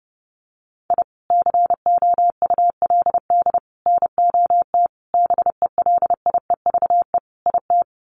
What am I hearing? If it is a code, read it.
Morse code, heard as a keyed tone, about 30 wpm, 700 Hz. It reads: I COULD NOT BELIEVE IT
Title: Sets of 5 Words Encore 30wpm